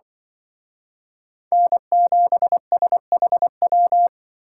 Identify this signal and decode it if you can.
Morse code, heard as a keyed tone, about 25 wpm, 700 Hz. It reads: N7SHW
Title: Speed Racing - USA 1x3 Call Signs 20wpm